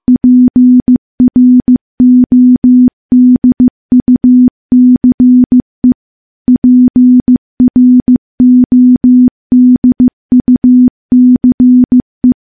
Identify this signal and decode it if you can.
Morse code, heard as a keyed tone, about 15 wpm, 250 Hz. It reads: PRODUCE PRODUCE
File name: morse-training-15WPM